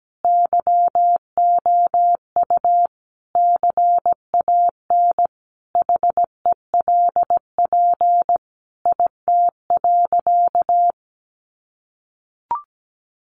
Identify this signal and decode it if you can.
Morse code, heard as a keyed tone, about 17 wpm, 700 Hz. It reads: YOU CAN HELP IT.
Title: Sentences from Top 1100 Words 17wpm